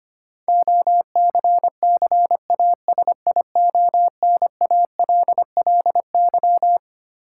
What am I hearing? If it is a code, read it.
Morse code, heard as a keyed tone, about 25 wpm, 700 Hz. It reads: OCCASIONALLY
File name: Twelve Letter Words - 25wpm